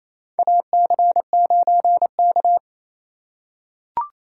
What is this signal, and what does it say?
Morse code, heard as a keyed tone, about 28 wpm, 700 Hz. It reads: AC9K
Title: USA Call Signs - 4 Characters - 2x Character Spacing - 28wpm